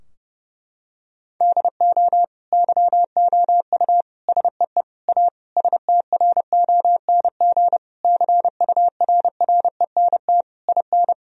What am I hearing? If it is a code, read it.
Morse code, heard as a keyed tone, about 30 wpm, 700 Hz. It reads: DO YOU SEE A STRONG CURRENT IN
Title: Sentences from Top 1000 Words - Review Part 4 - 30wpm